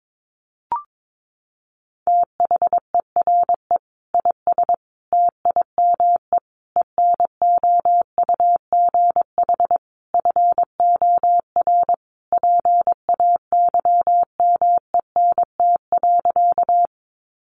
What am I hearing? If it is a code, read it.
Morse code, heard as a keyed tone, about 22 wpm, 700 Hz. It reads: THERE IS TIME ENOUGH FOR PAYMENT.
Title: Sentences from Top 1200 Words 22wpm